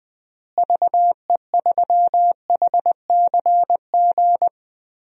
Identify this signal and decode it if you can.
Morse code, heard as a keyed tone, about 20 wpm, 700 Hz. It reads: VE3HCG
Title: DXCC-100 Call Signs - 6 Characters - Repeated 3x - 20wpm